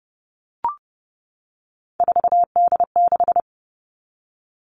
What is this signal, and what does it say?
Morse code, heard as a keyed tone, about 30 wpm, 700 Hz. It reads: 4D6
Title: All Cut Numbers - 3-Digit 30wpm